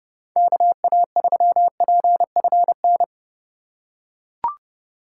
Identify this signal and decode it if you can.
Morse code, heard as a keyed tone, about 30 wpm, 700 Hz. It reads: KA3PFN
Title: USA Call Signs - 2x Character Spacing - 30wpm